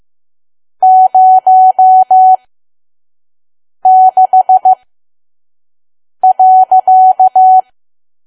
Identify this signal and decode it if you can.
Morse code, heard as a keyed tone, about 15 wpm, spaced as for 5 wpm, 750 Hz. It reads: 06.
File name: Archive / 5 WPM files / PM